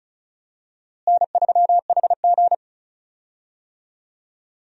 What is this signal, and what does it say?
Morse code, heard as a keyed tone, about 36 wpm, 700 Hz. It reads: N3HG
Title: USA Call Signs - 4 Characters - Repeated 3x - 35wpm